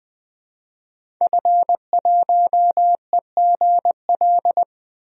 Morse code, heard as a keyed tone, about 20 wpm, 700 Hz. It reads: F1EGL
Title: DXCC-100 Call Signs - 5 Characters - Repeated 2x - 20wpm